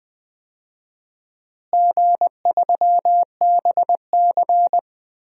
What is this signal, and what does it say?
Morse code, heard as a keyed tone, about 20 wpm, 700 Hz. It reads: G3BC
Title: DXCC-100 Call Signs - 4 Characters - Repeated 2x - 20wpm